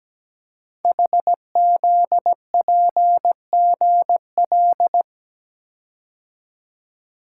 Morse code, heard as a keyed tone, about 17 wpm, 700 Hz. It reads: HZPGL
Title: Code Groups - 5 Letters - 17wpm